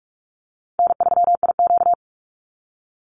Morse code, heard as a keyed tone, about 44 wpm, 700 Hz. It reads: N3IQ
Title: Easy Call Signs - 45wpm